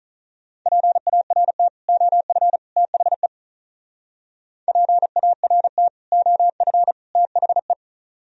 Morse code, heard as a keyed tone, about 44 wpm, 700 Hz. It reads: PART OF THE PART OF THE